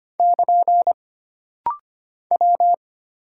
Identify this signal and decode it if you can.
Morse code, heard as a keyed tone, about 25 wpm, 700 Hz. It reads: <KN> W
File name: Single Character (Letter:Number:Punctuation:Prosign) - Rapid-Fire - 25wpm